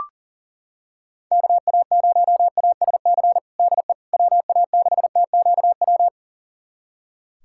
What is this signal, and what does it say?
Morse code, heard as a keyed tone, about 40 wpm, 700 Hz. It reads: KA0ASC DE WA6TQW
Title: USA Callsign DE Callsign - 6 Characters - 40wpm